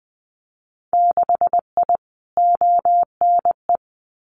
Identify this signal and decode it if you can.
Morse code, heard as a keyed tone, about 20 wpm, 700 Hz. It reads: 6I ONE
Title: Winter Field Day Exchanges Encore 20wpm